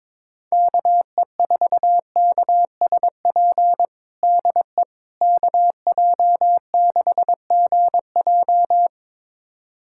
Morse code, heard as a keyed tone, about 22 wpm, 700 Hz. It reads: KE4KSP DE KJ6GJ
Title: USA Callsign DE Callsign 22wpm